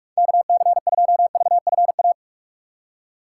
Morse code, heard as a keyed tone, about 46 wpm, 700 Hz. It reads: KC2VFA
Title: USA Call Signs - 6x Character Spacing - 45wpm